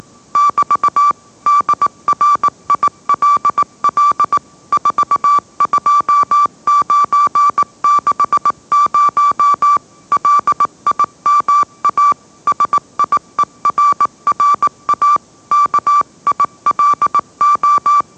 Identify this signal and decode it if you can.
Morse code, heard as a keyed tone, about 18 wpm, 1160 Hz. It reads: <BT> DRILL 42960 LIMA SIERRA KILO